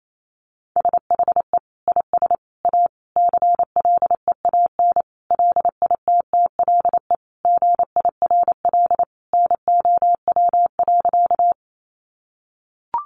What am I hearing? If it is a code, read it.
Morse code, heard as a keyed tone, about 28 wpm, 700 Hz. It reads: SHE IS A CLEAN LITTLE GIRL NOW.
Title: Sentences from Top 1100 Words 28wpm